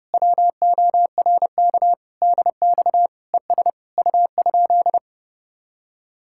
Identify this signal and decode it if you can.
Morse code, heard as a keyed tone, about 31 wpm, 700 Hz. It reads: WORK DX ES U?